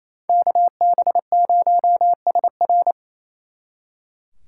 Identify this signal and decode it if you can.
Morse code, heard as a keyed tone, about 36 wpm, 700 Hz. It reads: KB0SR